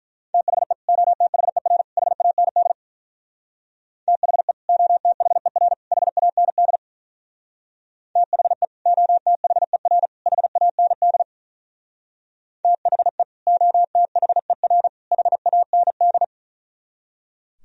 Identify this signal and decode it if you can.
Morse code, heard as a keyed tone, about 43 wpm, 700 Hz. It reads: THE OTHER HAND THE OTHER HAND THE OTHER HAND THE OTHER HAND
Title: Speed-Racing - Sets of 3 Words 35wpm